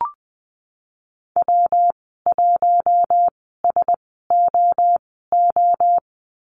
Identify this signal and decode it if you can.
Morse code, heard as a keyed tone, about 20 wpm, spaced as for 15 wpm, 700 Hz. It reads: W1SOO
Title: 122 Lesson 20g - Number 3 - Call Signs 20wpm